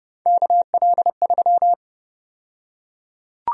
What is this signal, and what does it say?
Morse code, heard as a keyed tone, about 30 wpm, 700 Hz. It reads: KL3
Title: USA Call Sign Prefixes - 30wpm